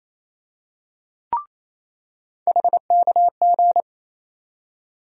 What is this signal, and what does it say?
Morse code, heard as a keyed tone, about 28 wpm, 700 Hz. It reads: HKG
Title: USA Call Sign Postfixes - 3 Letters 28wpm